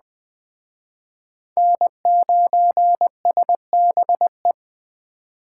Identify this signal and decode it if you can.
Morse code, heard as a keyed tone, about 21 wpm, 700 Hz. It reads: N9SBE
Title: Speed Racing - USA 1x3 Call Signs 17wpm